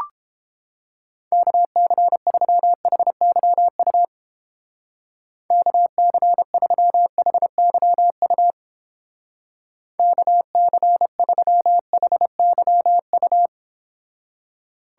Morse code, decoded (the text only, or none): KC3HYU KC3HYU KC3HYU